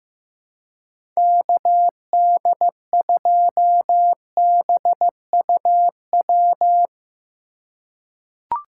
Morse code, heard as a keyed tone, about 15 wpm, 700 Hz. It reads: KD2BUW
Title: USA Call Signs - 2x Character Spacing - 15wpm